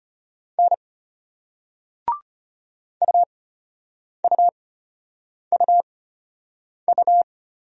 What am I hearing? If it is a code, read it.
Morse code, heard as a keyed tone, about 33 wpm, spaced as for 8 wpm, 700 Hz. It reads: N UUUU